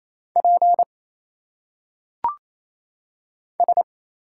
Morse code, heard as a keyed tone, about 27 wpm, 700 Hz. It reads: P S
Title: Single Letters - Phonetic Pronunciation 28wpm